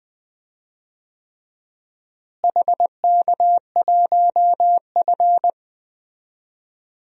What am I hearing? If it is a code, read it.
Morse code, heard as a keyed tone, about 20 wpm, 700 Hz. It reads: HK1F
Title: DXCC-100 Call Signs - 4 Characters - Repeated 3x - 20wpm